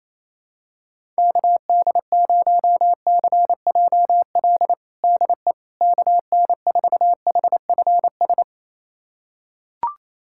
Morse code, heard as a keyed tone, about 28 wpm, 700 Hz. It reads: KD0CJL DE KN4HFS
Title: USA Callsign DE Callsign 28wpm